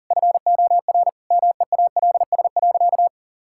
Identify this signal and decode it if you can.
Morse code, heard as a keyed tone, about 40 wpm, 700 Hz. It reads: FOR MEALS.
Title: Pangram Challenge v2 - 40wpm